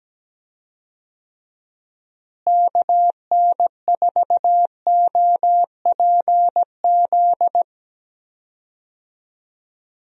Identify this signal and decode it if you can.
Morse code, heard as a keyed tone, about 17 wpm, 700 Hz. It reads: KN4OPZ
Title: USA Call Signs - 6 Characters - Repeated 2x - 17wpm